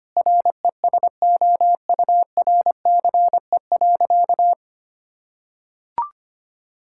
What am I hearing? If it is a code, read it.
Morse code, heard as a keyed tone, about 25 wpm, 700 Hz. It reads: RESOURCE.
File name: practice / 25wpm / Difficult Sentences with Hints - 25wpm